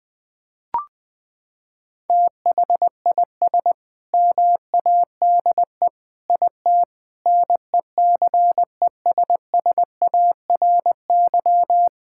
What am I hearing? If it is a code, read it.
Morse code, heard as a keyed tone, about 20 wpm, 700 Hz. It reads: THIS MADE IT NECESSARY
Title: Sentences from Top 1600 Words 20wpm